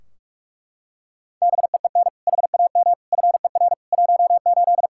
Morse code, heard as a keyed tone, about 42 wpm, 700 Hz. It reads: BEEN HAM FER 18
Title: Practice QSO - Exchange 3 - More Info - 45wpm